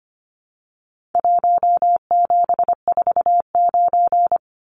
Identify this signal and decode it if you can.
Morse code, heard as a keyed tone, about 25 wpm, 700 Hz. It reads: 1749
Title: Numbers - Contest Style - 4-Digit 25wpm